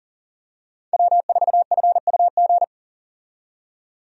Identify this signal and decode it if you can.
Morse code, heard as a keyed tone, about 40 wpm, 700 Hz. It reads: W4FUG
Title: USA Call Signs - 5 Characters - 40wpm